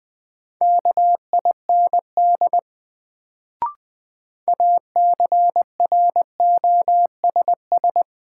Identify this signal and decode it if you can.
Morse code, heard as a keyed tone, about 20 wpm, 700 Hz. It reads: KIND ACROSS
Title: Sets of 2 Words - New Word Drill - 20wpm